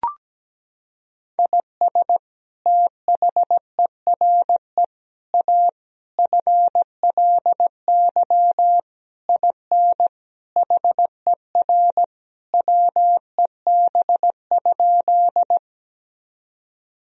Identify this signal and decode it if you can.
Morse code, heard as a keyed tone, about 17 wpm, 700 Hz. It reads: IS THERE A FLY IN HER WEB?
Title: Sentences from Top 1000 Words - Review Part 4 - 17wpm